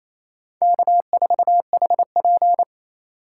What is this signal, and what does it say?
Morse code, heard as a keyed tone, about 36 wpm, 700 Hz. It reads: K4HP